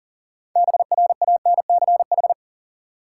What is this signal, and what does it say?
Morse code, heard as a keyed tone, about 40 wpm, 700 Hz. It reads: BRANCH
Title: Six Letter Words - 40wpm